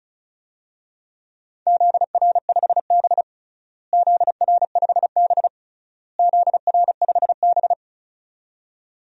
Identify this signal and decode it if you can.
Morse code, heard as a keyed tone, about 36 wpm, 700 Hz. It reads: ZR5B ZR5B ZR5B
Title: DXCC-100 Call Signs - 4 Characters - Repeated 3x - 35wpm